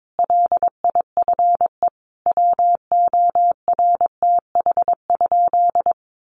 Morse code, heard as a keyed tone, about 22 wpm, 700 Hz. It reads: LIFE WORTH?
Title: Sentences from Top 800 Words 22wpm